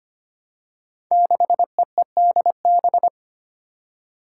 Morse code, heard as a keyed tone, about 25 wpm, 700 Hz. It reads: 6EEDB
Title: Five Challenging Letter-Number Combinations 25wpm